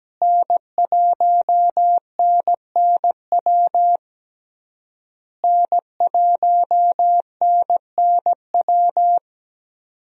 Morse code, heard as a keyed tone, about 17 wpm, 700 Hz. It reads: N1NNW N1NNW
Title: USA Call Signs - 5 Characters - Repeated 3x - 17wpm